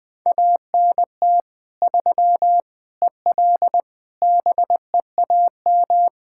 Morse code, heard as a keyed tone, about 20 wpm, 700 Hz. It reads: ANT 3 EL BEAM
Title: Practice QSO - Exchange 2 - Info - 20wpm